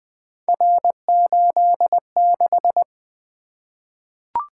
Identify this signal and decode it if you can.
Morse code, heard as a keyed tone, about 20 wpm, 700 Hz. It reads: R86
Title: Three Letter-Number Combinations - 20wpm